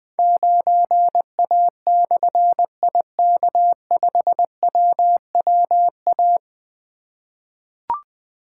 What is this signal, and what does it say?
Morse code, heard as a keyed tone, about 21 wpm, 700 Hz. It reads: 9A/IK5WWA
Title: Difficult Call Signs 20wpm